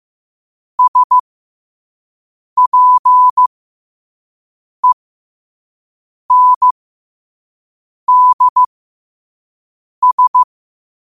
Morse code, heard as a keyed tone, about 15 wpm, spaced as for 5 wpm, 1000 Hz. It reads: SPENDS
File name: qotdfiles / 5wpm